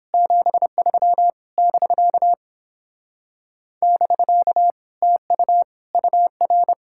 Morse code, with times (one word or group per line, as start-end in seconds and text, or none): 0.1-1.3 73
1.6-2.3 <BK>
3.8-4.7 <BK>
5.0-5.6 TU
5.9-6.7 UR